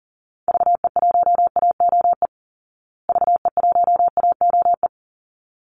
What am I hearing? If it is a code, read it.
Morse code, heard as a keyed tone, about 40 wpm, 700 Hz. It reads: VE1AOE VE1AOE
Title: Call Signs Repeated 3x - 40wpm